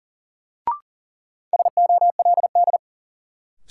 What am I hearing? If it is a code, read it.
Morse code, heard as a keyed tone, about 40 wpm, 700 Hz. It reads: SOLD
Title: Sentences from Top 1400 Words - New Word Drill - 40wpm